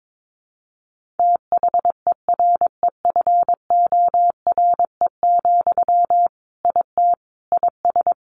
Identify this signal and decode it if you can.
Morse code, heard as a keyed tone, about 22 wpm, 700 Hz. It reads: THEREFORE, IT IS
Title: Sentences from Top 1500 Words 22wpm